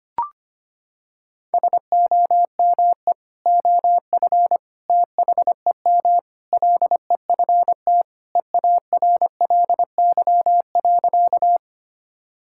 Morse code, heard as a keyed tone, about 25 wpm, 700 Hz. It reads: SOME OF THEM LEFT EARLY.
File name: practice / 25wpm / Sentences with 3-Word Phrases - 25wpm